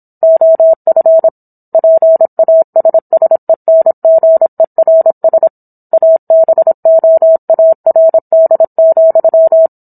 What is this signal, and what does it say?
Morse code, heard as a keyed tone, about 27 wpm, 650 Hz. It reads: OF PASSENGERS ABOARD,